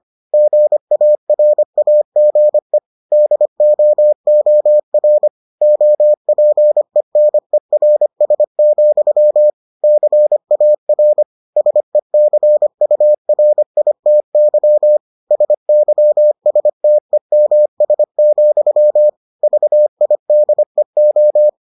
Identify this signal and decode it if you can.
Morse code, heard as a keyed tone, about 25 wpm, 600 Hz. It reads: GARAGE DOOR OPENERS, CAR SECURITY SYSTEMS, VIDEO